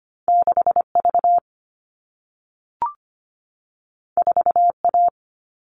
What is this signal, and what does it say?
Morse code, heard as a keyed tone, about 25 wpm, 700 Hz. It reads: 6V 4A
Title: All Cut Numbers - 2-Digit 25wpm